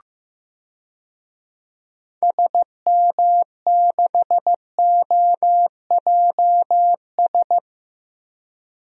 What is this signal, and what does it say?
Morse code, heard as a keyed tone, about 15 wpm, 700 Hz. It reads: SM6OJS
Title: DXCC-100 Call Signs - 6 Characters - 15wpm